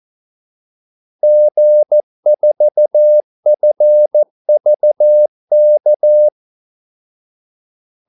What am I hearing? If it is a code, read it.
Morse code, heard as a keyed tone, about 14 wpm, 600 Hz. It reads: G4FVK
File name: Callsigns-14wpm